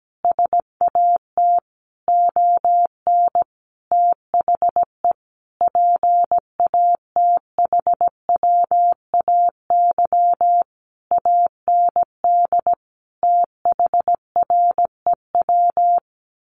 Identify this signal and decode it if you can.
Morse code, heard as a keyed tone, about 17 wpm, 700 Hz. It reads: SAT ON THE PATHWAY AND THREW
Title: A Sound Of Thunder 17wpm